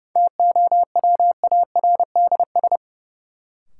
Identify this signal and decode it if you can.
Morse code, heard as a keyed tone, about 30 wpm, 700 Hz. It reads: TOWARDS
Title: Sentences from Top 1000 Words - Review Part 2 - New Word Drill - 30wpm